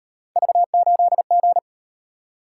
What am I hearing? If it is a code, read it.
Morse code, heard as a keyed tone, about 38 wpm, 700 Hz. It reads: V8G